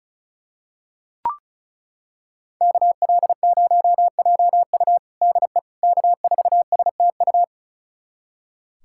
Morse code, heard as a keyed tone, about 36 wpm, 700 Hz. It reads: KL0JU DE K4STU
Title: USA Callsign DE Callsign - 5 Characters - 35wpm